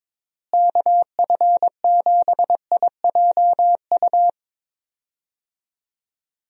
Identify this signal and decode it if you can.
Morse code, heard as a keyed tone, about 22 wpm, 700 Hz. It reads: KF7IJU
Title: USA Call Signs - 6 Characters - Repeated 3x - 22wpm